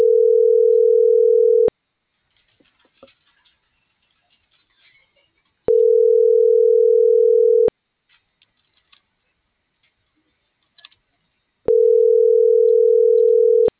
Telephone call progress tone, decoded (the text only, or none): ringback tone